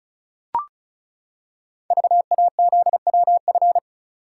Morse code, heard as a keyed tone, about 36 wpm, 690 Hz. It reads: VAZWF
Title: Code Groups - 5 Letter:Numbers - 35wpm